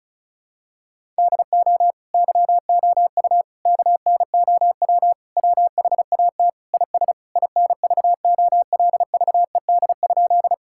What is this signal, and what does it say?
Morse code, heard as a keyed tone, about 36 wpm, 700 Hz. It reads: DO YOU KNOW WHAT IS INVOLVED?
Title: Sentences from Top 1000 Words - Review Part 5 - 35wpm